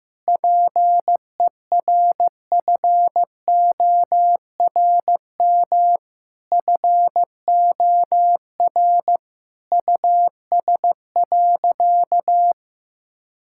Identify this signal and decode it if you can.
Morse code, heard as a keyed tone, about 15 wpm, 700 Hz. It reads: PERFORM FOR US.
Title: Sentences from Top 1000 Words - Review Part 5 - 15wpm